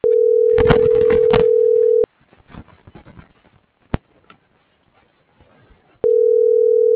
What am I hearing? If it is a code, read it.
Ringback tone (North America), 440 + 480 Hz.